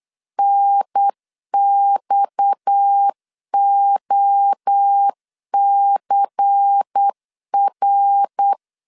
Morse code, heard as a keyed tone, about 8 wpm, 800 Hz. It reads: NXOCR
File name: LEZIONI / 6 PKX0 / 6 PKX 8wpm 3041